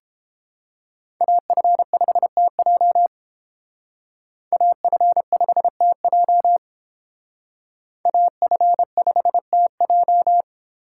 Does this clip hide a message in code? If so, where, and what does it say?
Yes: Morse code from 1.2 to 10.4 s, heard as a keyed tone, about 30 wpm, 700 Hz. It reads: AF5TJ AF5TJ AF5TJ